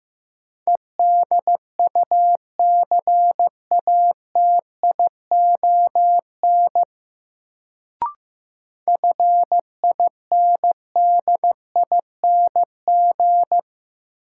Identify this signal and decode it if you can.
Morse code, heard as a keyed tone, about 15 wpm, 700 Hz. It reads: EDUCATION FINDING
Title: Sentences from Top 1100 Words - New Word Drill Alt - 15wpm